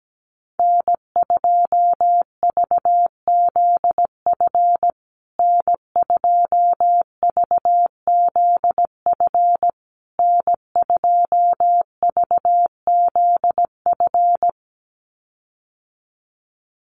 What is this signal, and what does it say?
Morse code, heard as a keyed tone, about 17 wpm, 700 Hz. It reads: N2VZF N2VZF N2VZF
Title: USA Call Signs Repeated 3x 17wpm